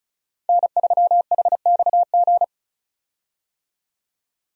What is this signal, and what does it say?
Morse code, heard as a keyed tone, about 36 wpm, 700 Hz. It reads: N3HXG
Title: USA Call Signs - 5 Characters - 35wpm